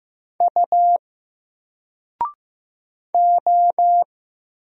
Morse code, heard as a keyed tone, about 15 wpm, 700 Hz. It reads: U O
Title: Single Letter-Number 15wpm